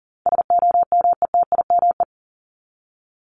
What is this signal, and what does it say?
Morse code, heard as a keyed tone, about 40 wpm, 700 Hz. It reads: SOMETIME
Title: ES:S Suffix - Top 100 Words with Base Words 40wpm